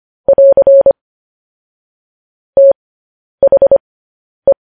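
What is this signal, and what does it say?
Morse code, heard as a keyed tone, about 25 wpm, spaced as for 10 wpm, 550 Hz. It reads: <AR> THE
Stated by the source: Morse code news headlines transmission.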